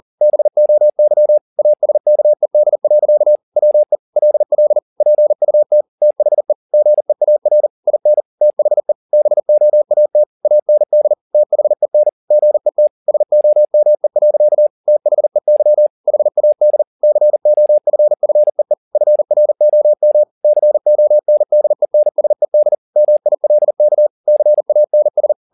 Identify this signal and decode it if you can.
Morse code, heard as a keyed tone, about 40 wpm, 600 Hz. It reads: BOY ASKED. WE LL PUT THE GEAR IN THE BOAT AND THEN GET SOME. THEY HAD COFFEE FROM CONDENSED MILK CANS